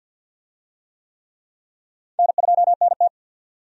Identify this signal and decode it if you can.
Morse code, heard as a keyed tone, about 52 wpm, 700 Hz. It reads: N2NT
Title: USA Call Signs - 4 Characters - 50wpm